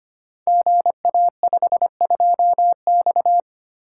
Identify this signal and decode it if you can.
Morse code, heard as a keyed tone, about 25 wpm, 700 Hz. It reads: GA52X
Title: Code Groups - 5 Letter:Numbers - 25wpm